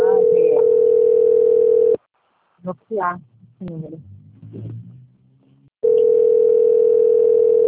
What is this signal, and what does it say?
Ringback tone (North America), 440 + 480 Hz, with speech.
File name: rg-1600-unknown- - - .18613